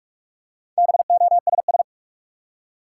Morse code, heard as a keyed tone, about 45 wpm, 700 Hz. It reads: BOSS
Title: Four Letter Words - 45wpm